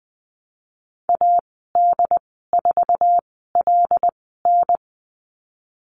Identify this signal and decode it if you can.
Morse code, heard as a keyed tone, about 19 wpm, spaced as for 14 wpm, 700 Hz. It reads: AD4LN
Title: 076 Lesson 13g - Letter L - Call Signs 20wpm